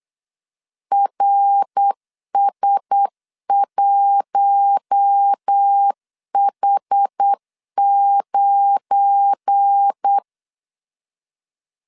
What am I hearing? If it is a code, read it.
Morse code, heard as a keyed tone, about 8 wpm, 800 Hz. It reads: RS1H9
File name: LEZIONI / 7 GF9Z / 7 GF9 8wpm 3545